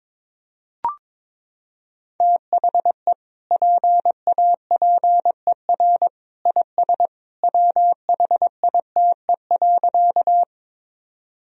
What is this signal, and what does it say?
Morse code, heard as a keyed tone, about 21 wpm, 700 Hz. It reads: THE PAPER IS WHITE.
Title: Sentences from Top 1000 Words - Review Part 1 - 22wpm